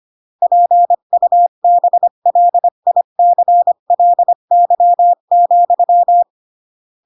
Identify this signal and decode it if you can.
Morse code, heard as a keyed tone, about 25 wpm, 700 Hz. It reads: PUBLICLY,